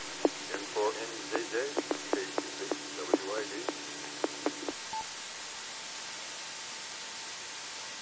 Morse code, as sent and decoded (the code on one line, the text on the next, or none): none